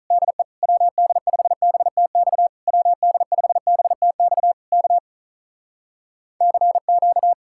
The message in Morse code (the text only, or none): DE WD5BTX WD5BTX K CQ